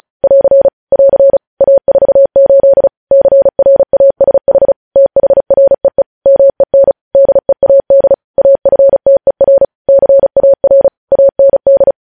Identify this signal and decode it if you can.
Morse code, heard as a keyed tone, about 36 wpm, 550 Hz. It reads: <AR> <AR> A48 CRASH THREE MEN DEAD AFTER CAR AND
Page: News Headlines in Morse Code at 35 WPM